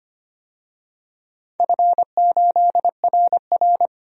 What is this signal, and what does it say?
Morse code, heard as a keyed tone, about 25 wpm, 700 Hz. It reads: F8RR
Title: DXCC-100 Call Signs - 4 Characters - 25wpm